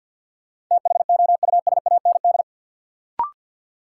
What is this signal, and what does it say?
Morse code, heard as a keyed tone, about 49 wpm, 700 Hz. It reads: THOUSAND
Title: Sentences from Top 600 Words - New Word Drill - 50wpm